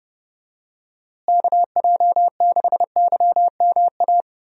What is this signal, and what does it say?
Morse code, heard as a keyed tone, about 32 wpm, 700 Hz. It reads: KJ6YMA
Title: USA Call Signs - 6 Characters - Repeated 3x - 30wpm